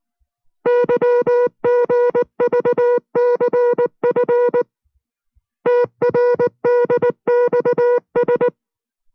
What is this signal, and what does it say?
Morse code, heard as a keyed tone, about 19 wpm, 490 Hz. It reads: YGVCF TRDXS